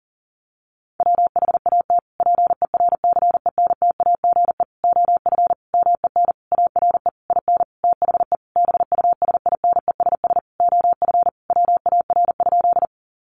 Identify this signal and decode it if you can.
Morse code, heard as a keyed tone, about 40 wpm, 700 Hz. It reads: WHAT PERCENTAGE OF MEN ARE IN THE BUSINESS OF WAR?
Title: Sentences from Top 1300 Words 40wpm